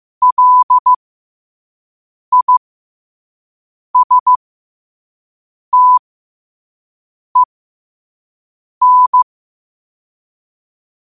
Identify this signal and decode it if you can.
Morse code, heard as a keyed tone, about 15 wpm, spaced as for 5 wpm, 1000 Hz. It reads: LISTEN
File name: qotdfiles / 5wpm